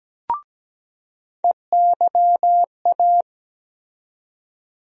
Morse code, heard as a keyed tone, about 17 wpm, 700 Hz. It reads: EYA
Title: USA Call Sign Postfixes - 3 Letters 17wpm